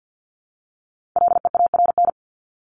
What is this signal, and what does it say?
Morse code, heard as a keyed tone, about 50 wpm, 700 Hz. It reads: LEARN
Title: Top 500 Words 50wpm